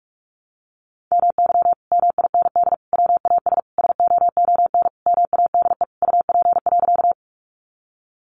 Morse code, heard as a keyed tone, about 44 wpm, 700 Hz. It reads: MY MIND WAS SOON MADE UP.
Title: Sentences from Top 500 Words 45wpm